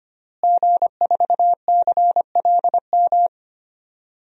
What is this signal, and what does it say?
Morse code, heard as a keyed tone, about 25 wpm, 700 Hz. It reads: G4CLM
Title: DXCC-100 Call Signs - 5 Characters - Repeated 3x - 25wpm